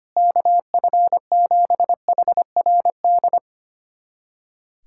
Morse code, heard as a keyed tone, about 25 wpm, 700 Hz. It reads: KF7HRD
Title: USA Callsign DE Callsign - 6 Characters - 25wpm